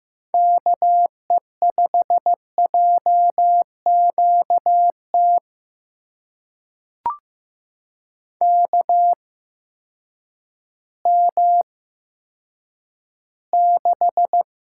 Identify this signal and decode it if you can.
Morse code, heard as a keyed tone, about 15 wpm, spaced as for 9 wpm, 700 Hz. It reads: KE5JQT KM6
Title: USA Call Signs - 6 Characters - 8x Character Spacing - 15wpm